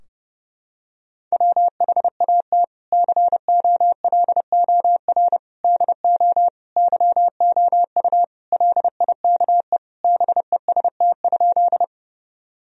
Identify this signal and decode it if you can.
Morse code, heard as a keyed tone, about 30 wpm, 700 Hz. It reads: WHAT COLOR DO YOU LIKE BEST?
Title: Sentences from Top 1000 Words - Review Part 2 - 30wpm